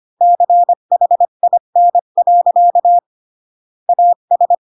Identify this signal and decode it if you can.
Morse code, heard as a keyed tone, about 25 wpm, 700 Hz. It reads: CHIN. AS
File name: Pinocchio / Pinocchio ch1 20 25 ik1hge